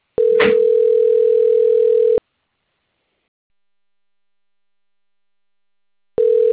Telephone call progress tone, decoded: ringback tone

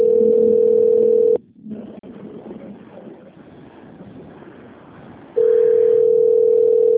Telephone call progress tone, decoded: ringback tone

